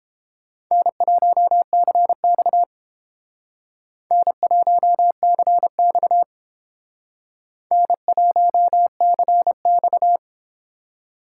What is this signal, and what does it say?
Morse code, heard as a keyed tone, about 29 wpm, 700 Hz. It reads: N1CX N1CX N1CX